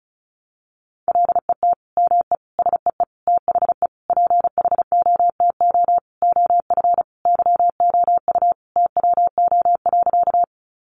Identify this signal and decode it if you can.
Morse code, heard as a keyed tone, about 34 wpm, 700 Hz. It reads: LET ME SEE THE PHOTO OF YOU TWO.
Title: Sentences from Top 1200 Words 35wpm